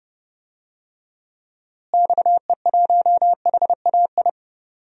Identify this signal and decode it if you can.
Morse code, heard as a keyed tone, about 30 wpm, 700 Hz. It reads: XE1HAI
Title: DXCC-100 Call Signs - 6 Characters - Repeated 3x - 30wpm